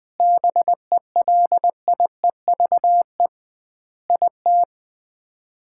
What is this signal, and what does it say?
Morse code, heard as a keyed tone, about 20 wpm, 700 Hz. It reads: BELIEVE IT
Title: 2x Word-Spacing - Sets of 5 Words - 20wpm